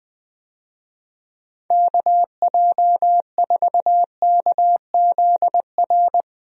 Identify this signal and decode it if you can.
Morse code, heard as a keyed tone, about 21 wpm, 700 Hz. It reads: KJ4KZR
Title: USA Call Signs - 6 Characters - 20wpm